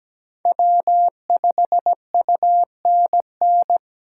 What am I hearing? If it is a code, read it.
Morse code, heard as a keyed tone, about 17 wpm, 700 Hz. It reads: W5UNN
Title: USA Call Signs - 5 Characters - 2x Character Spacing - 17wpm